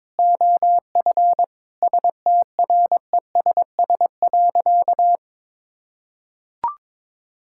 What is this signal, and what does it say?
Morse code, heard as a keyed tone, about 22 wpm, 700 Hz. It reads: OF STRESS.
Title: Sentences from Top 1000 Words - Review Part 6 - 22wpm